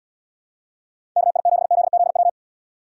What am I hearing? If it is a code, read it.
Morse code, heard as a keyed tone, about 69 wpm, 700 Hz. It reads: KI7ZZW